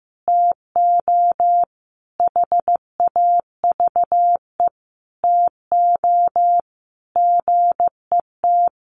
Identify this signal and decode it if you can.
Morse code, heard as a keyed tone, about 15 wpm, 700 Hz. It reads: TO HAVE TO GET
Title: Sets of 5 Words Encore2 15wpm